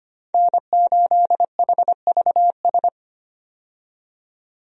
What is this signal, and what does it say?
Morse code, heard as a keyed tone, about 25 wpm, 700 Hz. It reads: N8HVS
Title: USA Call Signs - 5 Characters - 25wpm